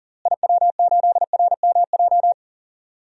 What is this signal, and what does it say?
Morse code, heard as a keyed tone, about 40 wpm, 700 Hz. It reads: IW8RMJ
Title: DXCC-100 Call Signs - 6 Characters - Repeated 2x - 40wpm